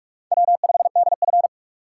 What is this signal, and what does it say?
Morse code, heard as a keyed tone, about 46 wpm, 700 Hz. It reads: W5DF
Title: USA Call Signs - 8x Character Spacing - 45wpm